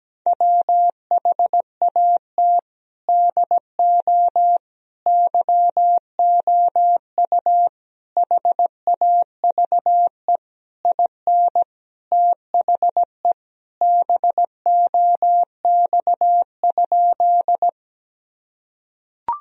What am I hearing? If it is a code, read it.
Morse code, heard as a keyed tone, about 17 wpm, 700 Hz. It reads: WHAT DO YOU HAVE IN THE BOX?
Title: Sentences from Top 1000 Words - Review Part 2 - 17wpm